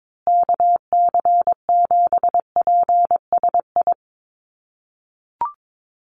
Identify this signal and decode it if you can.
Morse code, heard as a keyed tone, about 22 wpm, 700 Hz. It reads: KC7PSI
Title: USA Call Signs Repeated 3x 22wpm